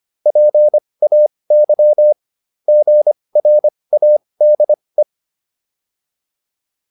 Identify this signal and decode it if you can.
Morse code, heard as a keyed tone, about 25 wpm, spaced as for 20 wpm, 600 Hz. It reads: PAY GRADE